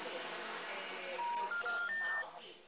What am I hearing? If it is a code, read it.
Special information tone, with noise.
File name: out- -2039- - - .39533